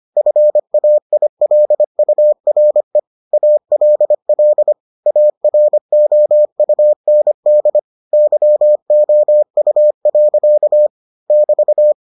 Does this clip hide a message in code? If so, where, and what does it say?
Yes: Morse code from 0.2 to 11.9 s, heard as a keyed tone, about 25 wpm, 600 Hz. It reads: FAILURE ALL AROUND YOU. <BT>